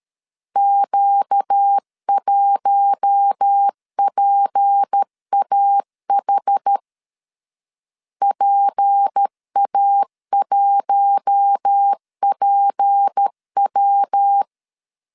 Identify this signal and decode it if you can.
Morse code, heard as a keyed tone, about 13 wpm, 800 Hz. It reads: Q1PAH PA1PW
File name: LEZIONI / 6 PKX0 / 6 P 12wpm 2033